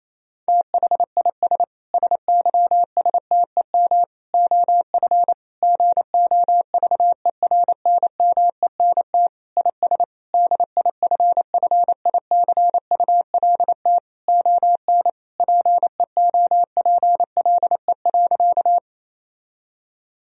Morse code, heard as a keyed tone, about 28 wpm, 700 Hz. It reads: THIS SYSTEM OF GOVERNMENT IS DIFFICULT ON PEOPLE.
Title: Sentences from Top 1000 Words - Review Part 3 - 28wpm